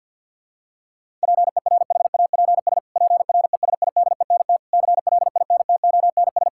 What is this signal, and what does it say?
Morse code, heard as a keyed tone, about 50 wpm, 700 Hz. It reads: PERHAPS PRESIDENT CLINTONS
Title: Pangram Challenge v2 - 50wpm